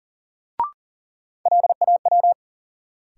Morse code, heard as a keyed tone, about 40 wpm, 700 Hz. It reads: LAW
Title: Sentences from Top 600 Words - New Word Drill - 40wpm